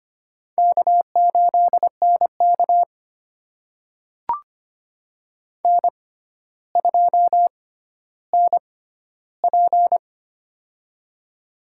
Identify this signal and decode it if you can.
Morse code, heard as a keyed tone, about 25 wpm, spaced as for 13 wpm, 700 Hz. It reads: K8NK N2NP
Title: USA Call Signs - 4 Characters - 6x Character Spacing - 25wpm